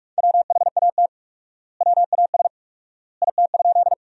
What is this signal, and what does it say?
Morse code, heard as a keyed tone, about 43 wpm, 700 Hz. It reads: WHAT WAS IT?
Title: 4x Word-Spacing - Sentences from Top 100 Words - 45wpm